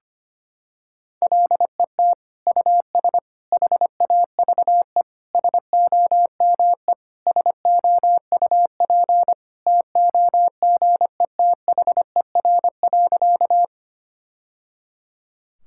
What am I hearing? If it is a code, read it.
Morse code, heard as a keyed tone, about 25 wpm, 700 Hz. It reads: LET US HAVE SOME SOUP TOGETHER.
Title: Sentences from Top 1000 Words - Review Part 6 - 25wpm